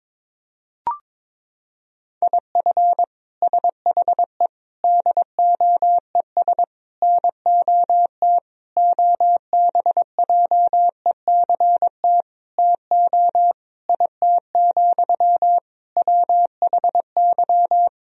Morse code, heard as a keyed tone, about 22 wpm, 700 Hz. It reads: IF SHE DOES NOT OBJECT TO IT, WHY
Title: Sentences from Top 1000 Words - Review Part 3 - 22wpm